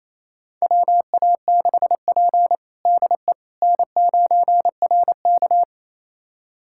Morse code, heard as a keyed tone, about 27 wpm, 700 Hz. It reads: WA6P DE N9RK
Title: USA Callsign DE Callsign - 4 Characters - 28wpm